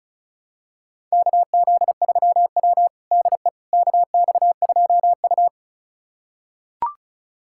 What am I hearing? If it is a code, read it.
Morse code, heard as a keyed tone, about 36 wpm, 700 Hz. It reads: KZ3W DE KX2U
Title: USA Callsign DE Callsign - 4 Characters - 35wpm